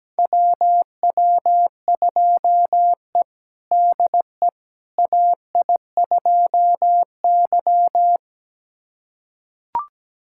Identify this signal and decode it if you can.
Morse code, heard as a keyed tone, about 17 wpm, 700 Hz. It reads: WW2E DE AI2Y
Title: USA Callsign DE Callsign - 4 Characters - 17wpm